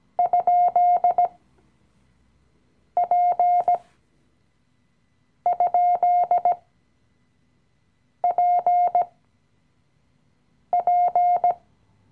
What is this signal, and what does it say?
Morse code, heard as a keyed tone, about 17 wpm, spaced as for 5 wpm, 710 Hz. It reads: ?P?PP
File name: CW-Online-Kurs / Anfänger / Lektion 018